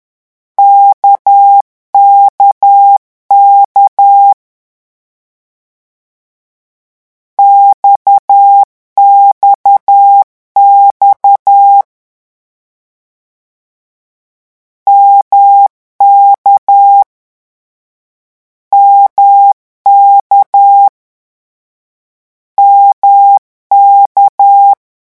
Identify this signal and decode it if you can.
Morse code, heard as a keyed tone, about 10 wpm, 800 Hz. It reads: KKK XXX MK MK MK
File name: morse mp3 / lecon1-vitesse 10